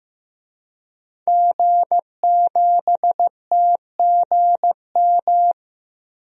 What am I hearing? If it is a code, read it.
Morse code, heard as a keyed tone, about 15 wpm, 700 Hz. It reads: G7TGM
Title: DXCC-100 Call Signs - 5 Characters - Repeated 2x - 15wpm